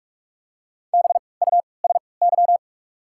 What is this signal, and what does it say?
Morse code, heard as a keyed tone, about 48 wpm, spaced as for 26 wpm, 700 Hz. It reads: BUSY
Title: 3x Character Spacing - Top 500 Words - 45wpm